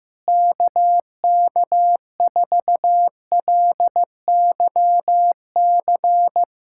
Morse code, heard as a keyed tone, about 15 wpm, 700 Hz. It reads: KK4LYC
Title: USA Callsign DE Callsign - 6 Characters - 15wpm